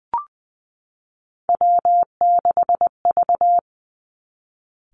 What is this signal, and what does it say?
Morse code, heard as a keyed tone, about 20 wpm, 700 Hz. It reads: W6V
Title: Three Challenging Letter-Number Combinations 20wpm